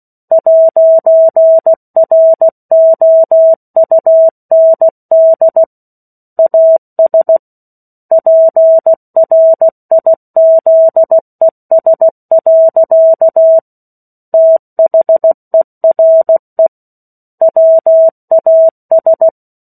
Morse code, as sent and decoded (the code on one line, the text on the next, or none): .----. .-. --- ..- -. -.. / .- ... / .--. .-. .. --.. ... .-.-.- / - .... .-. / .-- .- ...
'ROUND AS PRIZES. THERE WAS